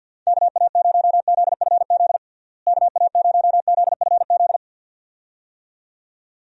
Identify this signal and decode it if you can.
Morse code, heard as a keyed tone, about 53 wpm, 700 Hz. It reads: KA0ZFZ KA0ZFZ
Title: USA Call Signs - 6 Characters - Repeated 2x - 50wpm